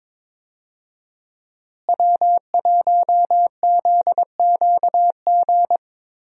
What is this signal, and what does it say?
Morse code, heard as a keyed tone, about 22 wpm, 700 Hz. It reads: W1ZQG
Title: USA Call Signs - 5 Characters - Repeated 3x - 22wpm